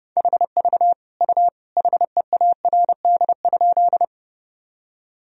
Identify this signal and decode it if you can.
Morse code, heard as a keyed tone, about 30 wpm, 700 Hz. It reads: HV U HEARD?
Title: Sentences from Top 400 Words and Abbreviations 30wpm